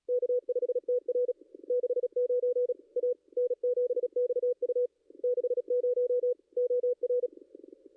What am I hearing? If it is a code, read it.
Morse code, heard as a keyed tone, about 37 wpm, 490 Hz. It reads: K5TR 69 A N7XU 60 OR
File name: contests / ss-cw-k5tr-op-n5rz -2200